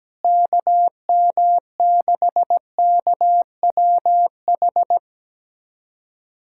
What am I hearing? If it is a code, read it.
Morse code, heard as a keyed tone, about 17 wpm, 700 Hz. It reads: KM6KWH
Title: USA Callsign DE Callsign 17wpm